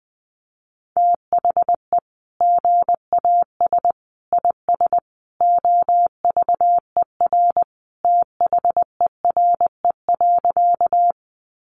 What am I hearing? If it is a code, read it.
Morse code, heard as a keyed tone, about 20 wpm, 700 Hz. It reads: THE GAS IS OVER THERE.
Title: Sentences from Top 1100 Words 20wpm